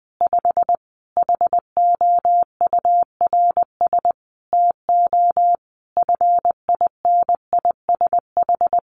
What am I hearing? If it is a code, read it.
Morse code, heard as a keyed tone, about 19 wpm, 700 Hz. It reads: 5 HOURS TO FINISH
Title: Additional Easy Phrases 20wpm